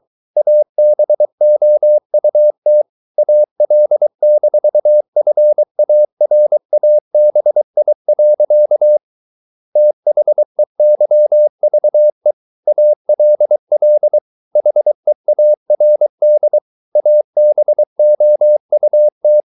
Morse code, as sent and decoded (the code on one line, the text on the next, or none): .- -... --- ..- - / .- .-.. -....- ..-. .- .-. .- -... .. .-.-.- / - .... -.-- ...- . / .- .-.. .-.. / .... .- .-. -.. / .- -... --- ..- -
ABOUT AL-FARABI. THEYVE ALL HEARD ABOUT